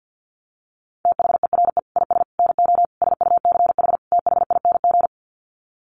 Morse code, heard as a keyed tone, about 50 wpm, 700 Hz. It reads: THERE IS NO SUCH THING
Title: Sets of 5 Words Encore 50wpm